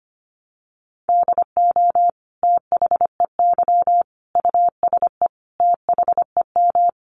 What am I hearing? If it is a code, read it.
Morse code, heard as a keyed tone, about 25 wpm, 700 Hz. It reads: DO THEY USE THEM
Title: Sentences from Top 1300 Words 25wpm